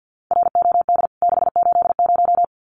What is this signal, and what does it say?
Morse code, heard as a keyed tone, about 50 wpm, 700 Hz. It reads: ROD 680
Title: CWT Exchange 50wpm